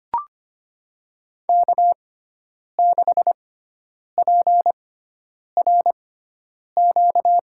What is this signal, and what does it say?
Morse code, heard as a keyed tone, about 25 wpm, spaced as for 9 wpm, 690 Hz. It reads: K6PRQ
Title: USA Call Signs - 5 Characters - 6x Character Spacing - 25wpm